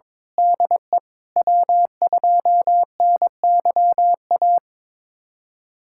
Morse code, heard as a keyed tone, about 21 wpm, 700 Hz. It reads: DE W2NYA
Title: USA Callsign DE Callsign - 5 Characters - 22wpm